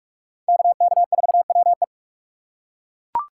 Morse code, heard as a keyed tone, about 39 wpm, 700 Hz. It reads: KK4WE